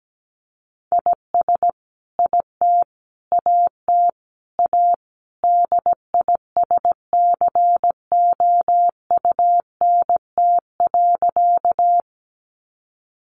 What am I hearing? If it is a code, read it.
Morse code, heard as a keyed tone, about 17 wpm, 700 Hz. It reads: IS IT AT A DISCOUNT.
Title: Sentences from Top 1200 Words 17wpm